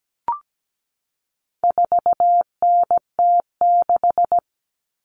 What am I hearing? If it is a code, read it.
Morse code, heard as a keyed tone, about 17 wpm, 700 Hz. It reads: 4NT6
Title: Common Cut Numbers - 4-Digit 17wpm